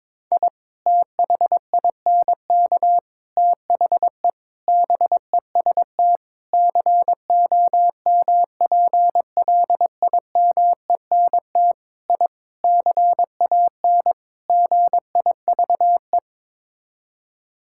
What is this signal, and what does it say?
Morse code, heard as a keyed tone, about 22 wpm, 700 Hz. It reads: I THINK THE BEST COMPLIMENT I CAN GIVE
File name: practice / 22wpm / Sets of 8 Words - 22wpm